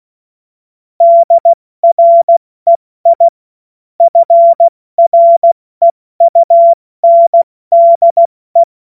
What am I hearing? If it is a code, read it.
Morse code, heard as a keyed tone, about 16 wpm, 680 Hz. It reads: DREI FREUNDE
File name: Morsekurs / Witze / Witz 21.November 2023 70 F80